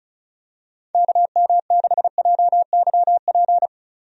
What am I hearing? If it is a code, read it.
Morse code, heard as a keyed tone, about 36 wpm, 700 Hz. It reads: KM6JYP
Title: USA Call Signs - 6 Characters - 35wpm